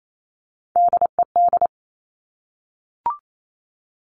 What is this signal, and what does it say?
Morse code, heard as a keyed tone, about 28 wpm, 700 Hz. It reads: DED
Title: Top Three Letter Combinations in Words 28wpm